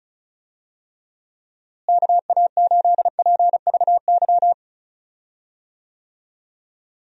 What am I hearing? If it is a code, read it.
Morse code, heard as a keyed tone, about 34 wpm, 700 Hz. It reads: KA8PVY
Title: USA Call Signs - 6 Characters - Repeated 3x - 35wpm